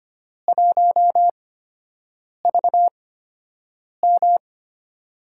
Morse code, heard as a keyed tone, about 25 wpm, spaced as for 7 wpm, 700 Hz. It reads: 1VM
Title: USA Call Signs - 4 Characters - 8x Character Spacing - 25wpm